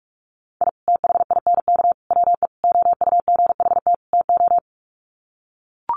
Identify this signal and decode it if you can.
Morse code, heard as a keyed tone, about 44 wpm, 700 Hz. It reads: I THINK WE OUGHT TO
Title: Sets of 5 Words Encore 45wpm